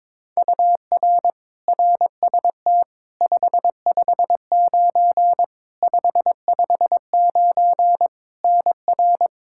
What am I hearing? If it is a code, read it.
Morse code, heard as a keyed tone, about 22 wpm, 700 Hz. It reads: UR RST 559 559 NR
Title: Practice QSO - Exchange 1 - Beginning - 22wpm